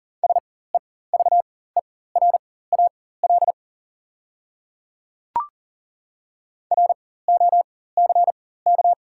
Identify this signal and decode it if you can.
Morse code, heard as a keyed tone, about 40 wpm, spaced as for 19 wpm, 690 Hz. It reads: SEVERAL ROCK
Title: 4x Character Spacing - Top 500 Words - 40wpm